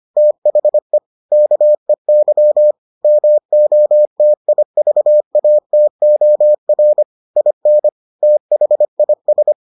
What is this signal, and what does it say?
Morse code, heard as a keyed tone, about 25 wpm, 600 Hz. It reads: THE KEY MOTIVATOR IN THIS